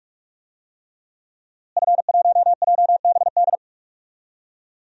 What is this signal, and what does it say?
Morse code, heard as a keyed tone, about 46 wpm, 700 Hz. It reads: F1JBD
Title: DXCC-100 Call Signs - 5 Characters - Repeated 3x - 45wpm